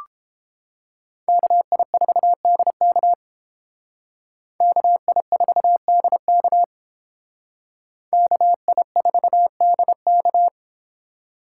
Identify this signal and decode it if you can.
Morse code, heard as a keyed tone, about 30 wpm, 700 Hz. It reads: KI4DK KI4DK KI4DK